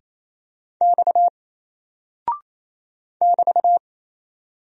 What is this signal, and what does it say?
Morse code, heard as a keyed tone, about 27 wpm, 700 Hz. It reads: X <BT>
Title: Single Character (Letter:Number:Punctuation:Prosign) 28wpm